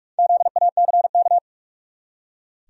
Morse code, heard as a keyed tone, about 45 wpm, 700 Hz. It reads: ZACK
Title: USA Common Names - Max 4 Letters - 45wpm